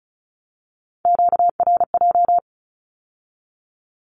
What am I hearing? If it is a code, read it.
Morse code, heard as a keyed tone, about 36 wpm, 700 Hz. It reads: QRJ
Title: Q-Codes with Spoken Meaning - 35wpm